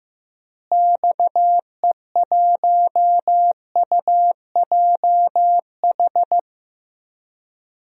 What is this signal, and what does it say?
Morse code, heard as a keyed tone, about 16 wpm, 700 Hz. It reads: XE1UJH
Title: DXCC-100 Call Signs - 6 Characters - Repeated 3x - 15wpm